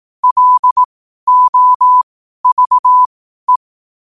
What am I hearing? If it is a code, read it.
Morse code, heard as a keyed tone, about 18 wpm, spaced as for 12 wpm, 1000 Hz. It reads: LOVE
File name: qotdfiles / 12wpm